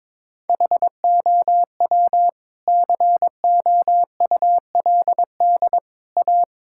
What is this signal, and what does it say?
Morse code, heard as a keyed tone, about 22 wpm, 700 Hz. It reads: HOW COULD A
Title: Sentences from Top 1000 Words - Review Part 6 - 22wpm